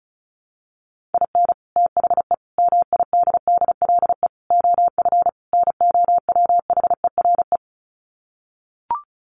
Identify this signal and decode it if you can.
Morse code, heard as a keyed tone, about 34 wpm, 700 Hz. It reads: IN THE MIDDLE OF NOWHERE
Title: Sets of 5 Words Encore 35wpm